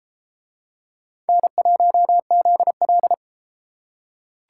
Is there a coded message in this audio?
Yes: Morse code from 1.3 to 3.1 s, heard as a keyed tone, about 32 wpm, 700 Hz. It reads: N1ZL